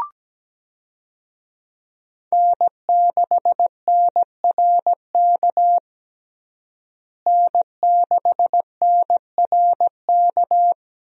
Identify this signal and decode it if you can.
Morse code, heard as a keyed tone, about 17 wpm, 700 Hz. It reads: N6NRK N6NRK
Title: USA Call Signs - 5 Characters - Repeated 2x - 17wpm